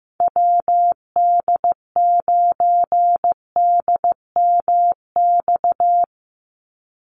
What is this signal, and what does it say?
Morse code, heard as a keyed tone, about 15 wpm, 700 Hz. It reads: WD9DMX
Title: USA Call Signs 15wpm